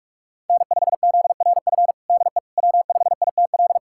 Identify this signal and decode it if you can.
Morse code, heard as a keyed tone, about 46 wpm, 700 Hz. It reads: N5ZRF DE W5ITL
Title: USA Callsign DE Callsign - 5 Characters - 45wpm